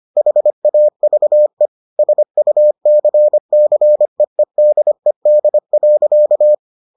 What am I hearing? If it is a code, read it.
Morse code, heard as a keyed tone, about 25 wpm, 600 Hz. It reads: HAVE SUCCEEDED.